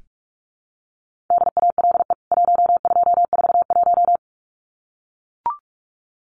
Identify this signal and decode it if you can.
Morse code, heard as a keyed tone, about 44 wpm, 700 Hz. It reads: DALE 1241
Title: CWT Exchange 45wpm